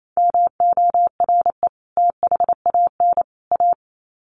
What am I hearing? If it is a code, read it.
Morse code, heard as a keyed tone, about 28 wpm, 700 Hz. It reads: MORE THAN A
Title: Sets of 3 Words 28wpm